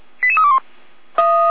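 Five-tone selective call (ZVEI), heard as quick runs of five tones, tones of about 70 ms. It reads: 88321
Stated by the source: ZVEI five-tone selective call received on SDR.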